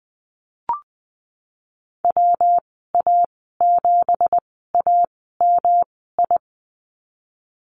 Morse code, heard as a keyed tone, about 20 wpm, spaced as for 15 wpm, 700 Hz. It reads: WA7AMI
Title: 179 Lesson 29g - Slash - Call Signs 20wpm